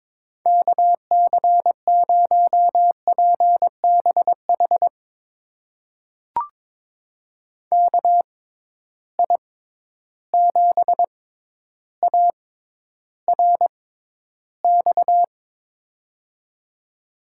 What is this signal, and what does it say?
Morse code, heard as a keyed tone, about 22 wpm, spaced as for 11 wpm, 700 Hz. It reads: KC0PBH KI7ARX
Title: USA Call Signs - 6 Characters - 6x Character Spacing - 22wpm